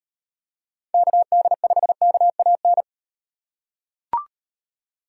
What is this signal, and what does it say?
Morse code, heard as a keyed tone, about 38 wpm, 700 Hz. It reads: KD5KAN